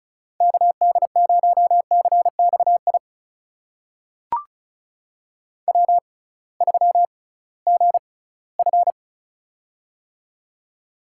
Morse code, heard as a keyed tone, about 36 wpm, spaced as for 20 wpm, 700 Hz. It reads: KD0CXI W3GF
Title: USA Call Signs - 6x Character Spacing - 35wpm